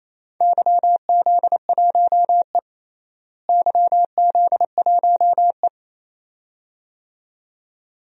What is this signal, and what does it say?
Morse code, heard as a keyed tone, about 28 wpm, 700 Hz. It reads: YZ1E YZ1E
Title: DXCC-100 Call Signs - 4 Characters - Repeated 2x - 28wpm